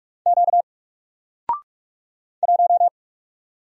Morse code, heard as a keyed tone, about 47 wpm, 700 Hz. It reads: Q 1
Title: Single Letter-Number Alt 45wpm